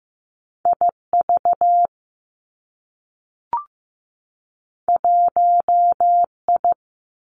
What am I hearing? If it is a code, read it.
Morse code, heard as a keyed tone, about 15 wpm, 700 Hz. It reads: I V 1 I
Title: Two Challenging Letter-Number Combinations 15wpm